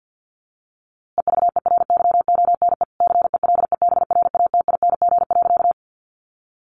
Morse code, heard as a keyed tone, about 50 wpm, 700 Hz. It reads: EVERYONE CELEBRATING.
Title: A Sound Of Thunder 50wpm